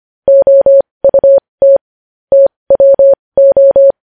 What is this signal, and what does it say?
Morse code, heard as a keyed tone, about 25 wpm, spaced as for 20 wpm, 550 Hz. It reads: OUT TWO
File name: podcast / News headlines-1x-25x20